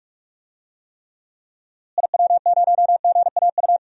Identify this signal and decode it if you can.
Morse code, heard as a keyed tone, about 42 wpm, 700 Hz. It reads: IW0GAU
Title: DXCC-100 Call Signs - 6 Characters - Repeated 3x - 45wpm